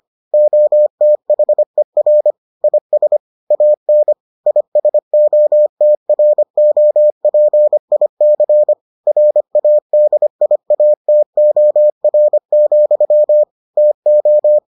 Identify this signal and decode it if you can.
Morse code, heard as a keyed tone, about 25 wpm, 600 Hz. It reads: OTHER IS AN ISOTROPIC RADIATOR, TO